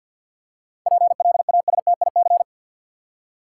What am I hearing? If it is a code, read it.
Morse code, heard as a keyed tone, about 52 wpm, 700 Hz. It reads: PLASTIC
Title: Seven Letter Words - 50wpm